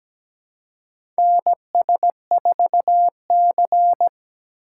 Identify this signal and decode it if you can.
Morse code, heard as a keyed tone, about 17 wpm, 700 Hz. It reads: NS4C
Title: USA Call Signs - 4 Characters - 4x Character Spacing - 17wpm